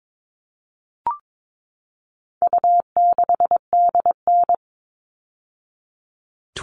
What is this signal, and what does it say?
Morse code, heard as a keyed tone, about 22 wpm, 700 Hz. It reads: U6DN
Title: All Cut Numbers - 4-Digit 22wpm